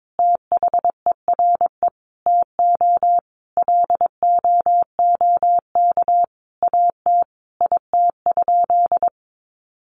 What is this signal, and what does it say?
Morse code, heard as a keyed tone, about 22 wpm, 700 Hz. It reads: THERE TO LOOK AT IT?
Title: Sentences from Top 100 Words 22wpm